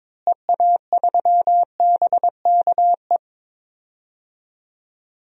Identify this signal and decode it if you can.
Morse code, heard as a keyed tone, about 21 wpm, 700 Hz. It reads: EA3BKE
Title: DXCC-100 Call Signs - 6 Characters - Repeated 3x - 22wpm